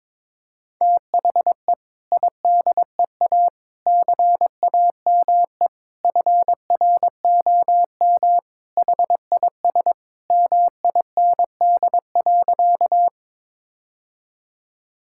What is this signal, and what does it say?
Morse code, heard as a keyed tone, about 22 wpm, 700 Hz. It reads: THE IDEA CAME FROM HIS MIND.
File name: practice / 22wpm / Sentences with 2-Word Phrases - 22wpm